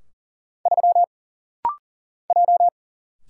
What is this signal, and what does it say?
Morse code, heard as a keyed tone, about 40 wpm, 700 Hz. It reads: 3 J
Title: Single Letter-Number - Rapid-Fire 40wpm